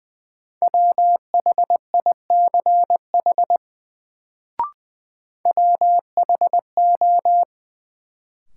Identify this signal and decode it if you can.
Morse code, heard as a keyed tone, about 21 wpm, 700 Hz. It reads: WHICH WHO
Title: Sets of 2 Words - New Word Drill - 20wpm